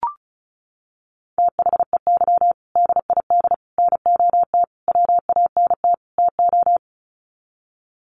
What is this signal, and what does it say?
Morse code, heard as a keyed tone, about 34 wpm, 700 Hz. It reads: THEY DID NOT WANT TO
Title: Sets of 5 Words 35wpm